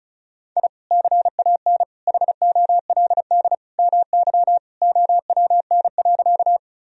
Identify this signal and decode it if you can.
Morse code, heard as a keyed tone, about 36 wpm, 700 Hz. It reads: I CAN HOLD MY OWN.
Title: Sentences from Top 1000 Words - Review Part 2 - 35wpm